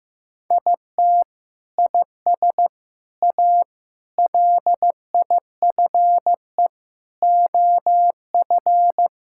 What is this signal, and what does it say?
Morse code, heard as a keyed tone, about 15 wpm, 700 Hz. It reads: IT IS A LIFE OF
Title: Sentences from Top 1000 Words - Review Part 5 - 15wpm